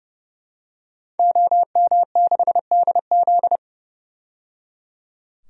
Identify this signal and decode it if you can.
Morse code, heard as a keyed tone, about 30 wpm, 700 Hz. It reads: OM6DZ
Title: DXCC-100 Call Signs - 5 Characters - 30wpm